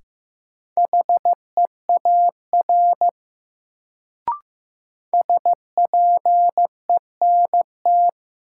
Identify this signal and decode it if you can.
Morse code, heard as a keyed tone, about 15 wpm, 700 Hz. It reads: HEAR SPENT
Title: Sets of 5 Words - New Word Drill - 15wpm